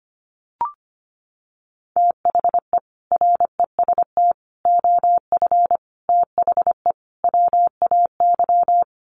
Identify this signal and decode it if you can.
Morse code, heard as a keyed tone, about 25 wpm, 700 Hz. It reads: THE REST OF THE WAY
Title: Sets of 5 Words Encore 25wpm